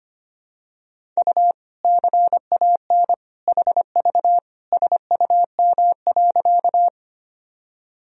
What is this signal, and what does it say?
Morse code, heard as a keyed tone, about 25 wpm, 700 Hz. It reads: U CAN HV SUM.
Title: Sentences from Top 100 Words and Abbreviations 25wpm